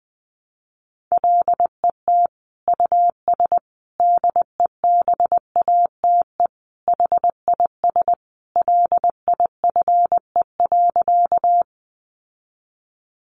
Morse code, heard as a keyed tone, about 20 wpm, 700 Hz. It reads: LET US DEBATE HIS LIFE.
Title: Sentences from Top 1100 Words 20wpm